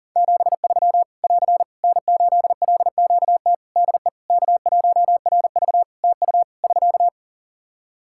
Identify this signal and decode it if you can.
Morse code, heard as a keyed tone, about 40 wpm, 700 Hz. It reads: 73 <AR> N8LQT DE K1RV TU <SK>
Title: K1USN SST QSO Practice 40wpm